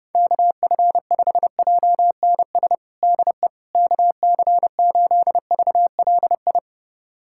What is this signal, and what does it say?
Morse code, heard as a keyed tone, about 32 wpm, 700 Hz. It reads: KF5JNS DE KC8VLI
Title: USA Callsign DE Callsign - 6 Characters - 30wpm